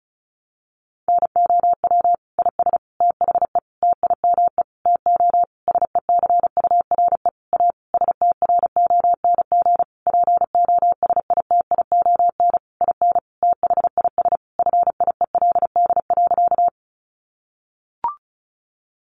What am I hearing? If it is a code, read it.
Morse code, heard as a keyed tone, about 34 wpm, 700 Hz. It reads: NOW IS THE TIME TO SECURE A STRONG POSITION IN THIS FIELD.
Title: Sentences from Top 1200 Words 35wpm